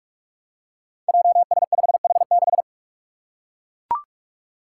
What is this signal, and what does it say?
Morse code, heard as a keyed tone, about 42 wpm, 700 Hz. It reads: JS5H6
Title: Five Challenging Letter-Number Combinations 45wpm